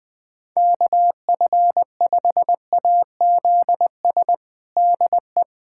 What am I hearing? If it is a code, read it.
Morse code, heard as a keyed tone, about 20 wpm, 700 Hz. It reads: KF5AZS DE
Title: USA Callsign DE Callsign - 6 Characters - 20wpm